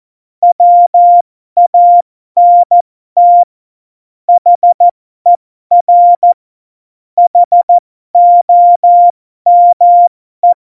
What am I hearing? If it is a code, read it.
Morse code, heard as a keyed tone, about 14 wpm, spaced as for 12 wpm, 700 Hz. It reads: WANT HER HOME
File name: Morsekurs / Witze